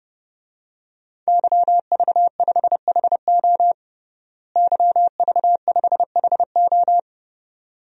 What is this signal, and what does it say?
Morse code, heard as a keyed tone, about 30 wpm, 700 Hz. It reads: YV5HO YV5HO
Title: DXCC-100 Call Signs - 5 Characters - Repeated 2x - 30wpm